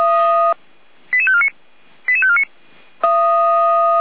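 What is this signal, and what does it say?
Five-tone selective call (ZVEI), heard as quick runs of five tones, tones of about 70 ms. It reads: 88549 88549
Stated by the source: ZVEI five-tone selective call received on an SDR, recorded by feuerwehr-orlamuende (fire brigade Orlamünde).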